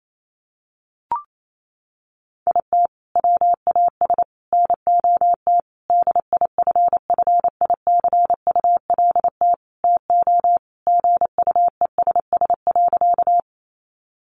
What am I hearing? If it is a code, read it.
Morse code, heard as a keyed tone, about 28 wpm, 700 Hz. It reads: IT WAS NOT DIFFICULT TO GUESS.
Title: Sentences from Top 1500 Words 28wpm